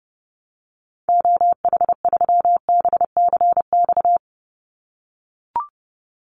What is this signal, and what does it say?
Morse code, heard as a keyed tone, about 30 wpm, 700 Hz. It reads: OH3BCX
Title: Call Signs - 30wpm